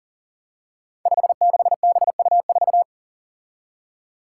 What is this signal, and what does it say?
Morse code, heard as a keyed tone, about 40 wpm, 700 Hz. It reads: 56BU4
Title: Five Challenging Letter-Number Combinations 40wpm